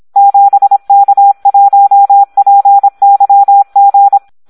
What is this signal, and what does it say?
Morse code, heard as a keyed tone, about 26 wpm, 800 Hz. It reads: 7K1PYG
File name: Audio practica CW / SD 1 26